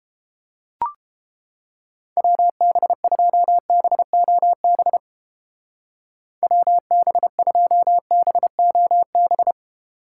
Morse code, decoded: WB2BOB WB2BOB

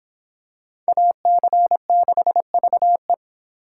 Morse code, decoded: AC6VE

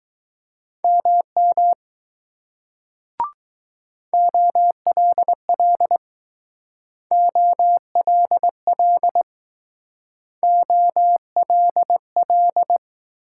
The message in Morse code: MM OLL OLL OLL

